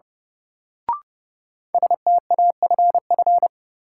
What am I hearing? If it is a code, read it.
Morse code, heard as a keyed tone, about 32 wpm, 700 Hz. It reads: STAFF
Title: Sets of 5 Words Encore - New Word Drill - 30wpm